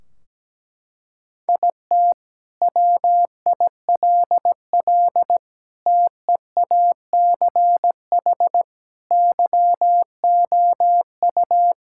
Morse code, heard as a keyed tone, about 17 wpm, 700 Hz. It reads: IT WILL TEACH YOU
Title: Sentences from Top 1600 Words 17wpm